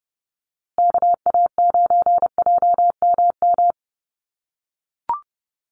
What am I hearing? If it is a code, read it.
Morse code, heard as a keyed tone, about 30 wpm, 700 Hz. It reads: KA9JMM
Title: USA Call Signs Repeated 2x 30wpm